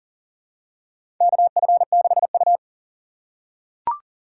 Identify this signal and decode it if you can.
Morse code, heard as a keyed tone, about 40 wpm, 700 Hz. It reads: KF6U
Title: USA Call Signs - 4 Characters - 2x Character Spacing - 40wpm